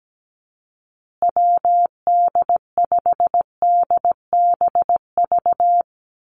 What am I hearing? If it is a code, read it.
Morse code, heard as a keyed tone, about 17 wpm, 700 Hz. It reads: WD5DBV
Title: Call Signs Repeated 2x - 17wpm